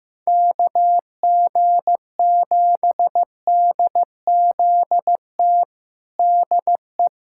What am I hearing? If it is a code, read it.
Morse code, heard as a keyed tone, about 15 wpm, 700 Hz. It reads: KG7DZT DE
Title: USA Callsign DE Callsign - 6 Characters - 15wpm